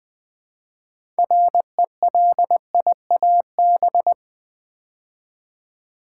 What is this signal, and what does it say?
Morse code, heard as a keyed tone, about 20 wpm, 700 Hz. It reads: RELIAB
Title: Build A Word - 2 Letters At A Time - 8-Letter Words - 20wpm